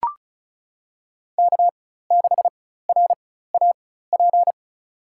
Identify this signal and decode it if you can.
Morse code, heard as a keyed tone, about 34 wpm, spaced as for 16 wpm, 700 Hz. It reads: K6RAP
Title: USA Call Signs - 4x Character Spacing - 35wpm